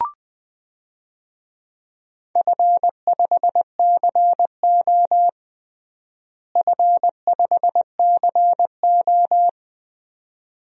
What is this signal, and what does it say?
Morse code, heard as a keyed tone, about 20 wpm, 700 Hz. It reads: F5CO F5CO
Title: DXCC-100 Call Signs - 4 Characters - Repeated 2x - 20wpm